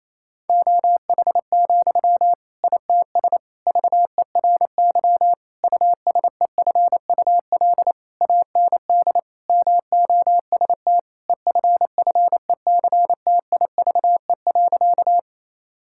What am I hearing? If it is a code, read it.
Morse code, heard as a keyed tone, about 28 wpm, 700 Hz. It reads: OH, ITS VERY USEFUL AND MOST EFFECTIVE.
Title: Sentences from Top 1000 Words - Review Part 6 - 28wpm